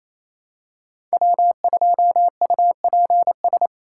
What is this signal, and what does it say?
Morse code, heard as a keyed tone, about 28 wpm, 700 Hz. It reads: W2UPS
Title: USA Call Signs - 5 Characters - 2x Character Spacing - 28wpm